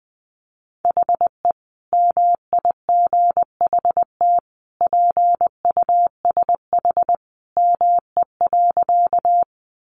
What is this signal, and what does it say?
Morse code, heard as a keyed tone, about 20 wpm, 700 Hz. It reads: HE MIGHT PUSH ME.
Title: Sentences from Top 1300 Words 20wpm